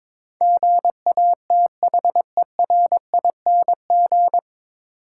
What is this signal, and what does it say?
Morse code, heard as a keyed tone, about 23 wpm, 700 Hz. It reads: GATHERING
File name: 9 Letter Words - First 3 Letters Given - 22wpm